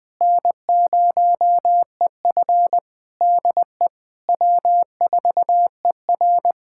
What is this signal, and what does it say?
Morse code, heard as a keyed tone, about 20 wpm, 700 Hz. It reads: N0EF DE W4ER
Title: Practice QSO - Exchange 4 - Ending - 20wpm